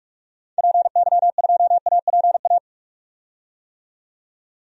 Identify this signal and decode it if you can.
Morse code, heard as a keyed tone, about 46 wpm, 700 Hz. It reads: PY2APA
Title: DXCC-100 Call Signs - 6 Characters - 45wpm